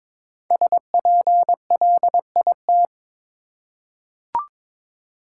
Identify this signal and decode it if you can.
Morse code, heard as a keyed tone, about 22 wpm, 700 Hz. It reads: SPLIT
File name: 5 Letter Words - First 3 Letters Given - 22wpm